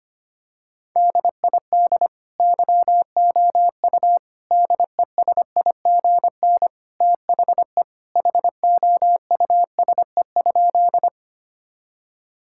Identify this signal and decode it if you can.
Morse code, heard as a keyed tone, about 25 wpm, 700 Hz. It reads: DID YOU DESIGN THE HOUSE?
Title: Sentences from Top 1000 Words - Review Part 5 - 25wpm